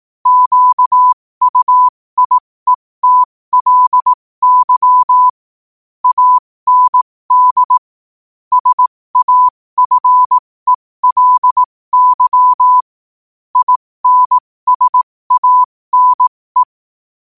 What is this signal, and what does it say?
Morse code, heard as a keyed tone, about 18 wpm, spaced as for 15 wpm, 1000 Hz. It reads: QUIETLY AND SAFELY INSANE
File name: qotdfiles / 15wpm